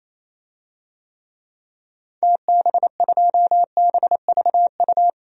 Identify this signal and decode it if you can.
Morse code, heard as a keyed tone, about 27 wpm, 700 Hz. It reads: TB2BVU
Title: DXCC-100 Call Signs - 6 Characters - 28wpm